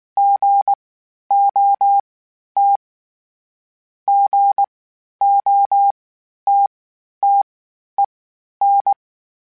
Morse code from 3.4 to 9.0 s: GOTTEN

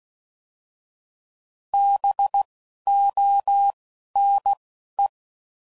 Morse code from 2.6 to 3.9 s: O

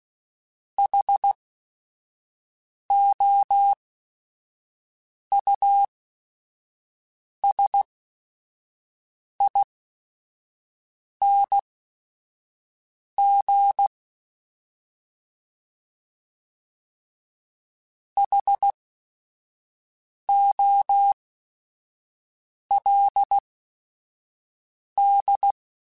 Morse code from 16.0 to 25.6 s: HOLD

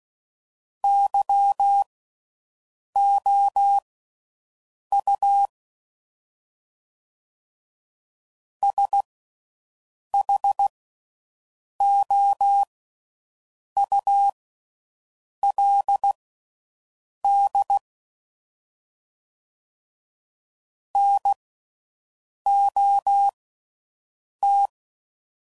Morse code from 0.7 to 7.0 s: YOU

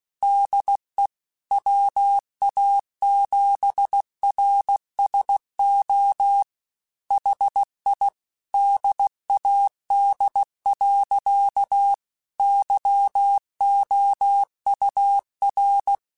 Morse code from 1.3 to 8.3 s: WA7RSO HI

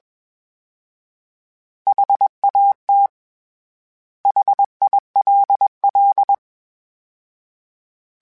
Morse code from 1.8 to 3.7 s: HAT

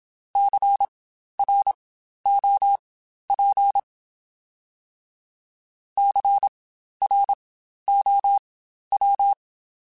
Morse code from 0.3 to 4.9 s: CROP

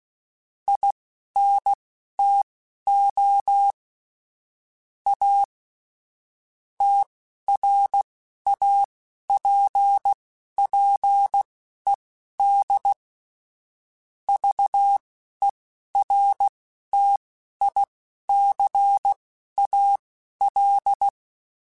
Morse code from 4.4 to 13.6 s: A TRAPPED

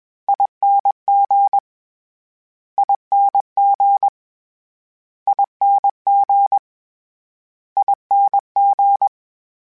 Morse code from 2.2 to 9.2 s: ING ING ING

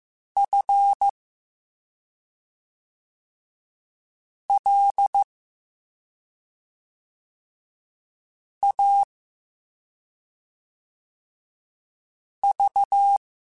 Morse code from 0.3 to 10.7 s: F L A

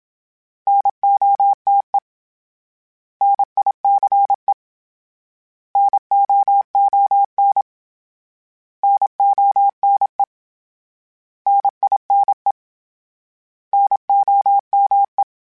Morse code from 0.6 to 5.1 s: NOTE NICE

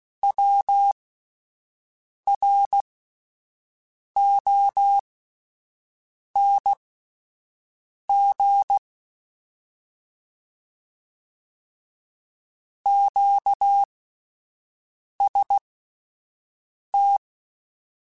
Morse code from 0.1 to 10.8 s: WRONG

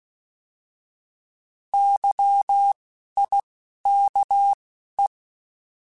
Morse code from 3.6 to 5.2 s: K E

